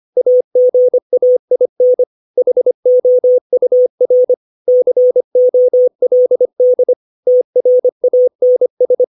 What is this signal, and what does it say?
Morse code, heard as a keyed tone, about 25 wpm, 500 Hz. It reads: AGAIN HOUR COLD TRANS
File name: 632 words / 632 Words 25WPM